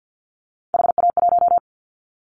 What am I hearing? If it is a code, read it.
Morse code, heard as a keyed tone, about 50 wpm, 700 Hz. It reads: HA1
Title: Callsign Prefixes for Top 25 DXCC - Repeated 3x - 50wpm